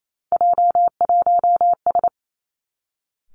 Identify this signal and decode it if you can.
Morse code, heard as a keyed tone, about 28 wpm, 700 Hz. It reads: J1S
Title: Three Challenging Letter-Number Combinations 28wpm